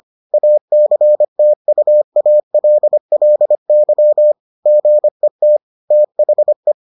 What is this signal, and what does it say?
Morse code, heard as a keyed tone, about 25 wpm, 600 Hz. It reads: ACTUALLY GET THE